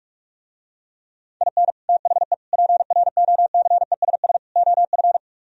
Morse code, heard as a keyed tone, about 47 wpm, 700 Hz. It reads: IN THE PROCESS OF